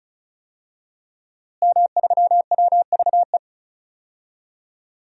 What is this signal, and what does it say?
Morse code, heard as a keyed tone, about 36 wpm, 700 Hz. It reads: M3WVE
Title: DXCC-100 Call Signs - 5 Characters - Repeated 3x - 35wpm